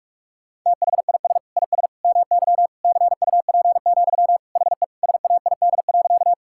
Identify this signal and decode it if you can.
Morse code, heard as a keyed tone, about 47 wpm, 700 Hz. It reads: THIS IS MY CUP, HE SAID.
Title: Sentences from Top 1600 Words 45wpm